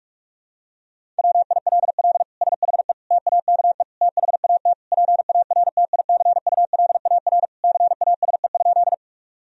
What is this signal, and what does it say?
Morse code, heard as a keyed tone, about 41 wpm, 700 Hz. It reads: WILL SHE TAKE THAT PARTICULAR CASE?
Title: Sentences from Top 1000 Words - Review Part 4 - 45wpm